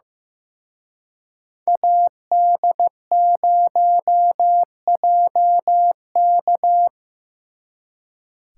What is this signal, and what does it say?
Morse code, heard as a keyed tone, about 18 wpm, 700 Hz. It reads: AD0JK
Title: Speed Racing - USA 2x2 Call Signs 15wpm